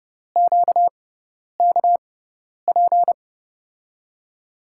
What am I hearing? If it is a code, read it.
Morse code, heard as a keyed tone, about 32 wpm, spaced as for 10 wpm, 700 Hz. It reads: QKP
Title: USA Call Signs - 5 Characters - 6x Character Spacing - 30wpm